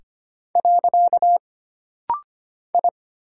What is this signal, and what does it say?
Morse code, heard as a keyed tone, about 25 wpm, 700 Hz. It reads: . I
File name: Single Character (Letter:Number:Punctuation:Prosign) - Rapid-Fire - 25wpm